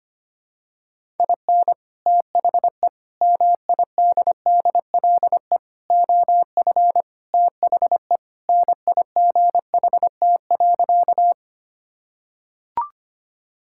Morse code, heard as a keyed tone, about 25 wpm, 700 Hz. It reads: IN THE MIDDLE OF THE NIGHT.
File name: practice / 25wpm / Sentences with 3-Word Phrases - 25wpm